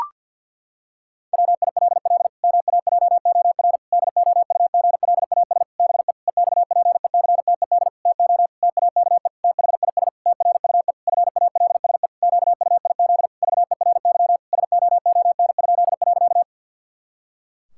Morse code, heard as a keyed tone, about 50 wpm, 700 Hz. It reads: WILL MAJOR DOUGLAS BE EXPECTED TO TAKE THIS TRUE FALSE QUIZ VERY SOON?.
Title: Pangram Challenge v2 - 50wpm